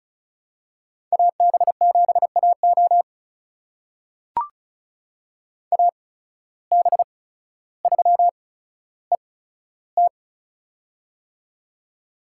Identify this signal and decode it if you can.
Morse code, heard as a keyed tone, about 36 wpm, spaced as for 15 wpm, 700 Hz. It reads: AB7AO AB3ET
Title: USA Call Signs - 5 Characters - 8x Character Spacing - 35wpm